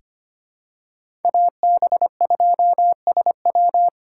Morse code, heard as a keyed tone, about 25 wpm, 700 Hz. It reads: AB2SW
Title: USA Call Signs - 3x Character Spacing - 25wpm